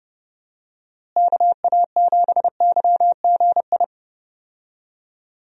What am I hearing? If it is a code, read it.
Morse code, heard as a keyed tone, about 30 wpm, 700 Hz. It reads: KA7YGI
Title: USA Call Signs - 6 Characters - Repeated 3x - 30wpm